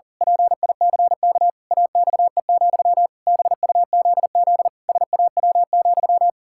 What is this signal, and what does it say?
Morse code, heard as a keyed tone, about 40 wpm, 700 Hz. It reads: PICK AXE, BUZZ SAW,
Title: Pangram Challenge v2 - 40wpm